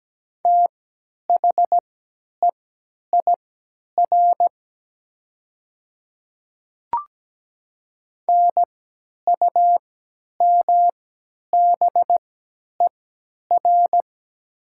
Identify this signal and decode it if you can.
Morse code, heard as a keyed tone, about 17 wpm, spaced as for 10 wpm, 700 Hz. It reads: THEIR NUMBER
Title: 3x Character Spacing - Top 200 Words - 17wpm